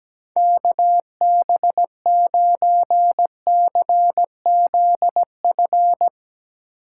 Morse code, heard as a keyed tone, about 17 wpm, 700 Hz. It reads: KB9CZF
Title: Speed Racing - USA 2x3 Call Signs 17wpm